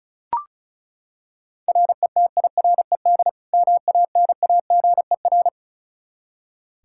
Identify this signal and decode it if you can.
Morse code, heard as a keyed tone, about 36 wpm, 700 Hz. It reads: RETIRED MANAGER
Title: Practice QSO - Exchange 3 - More Info - 35wpm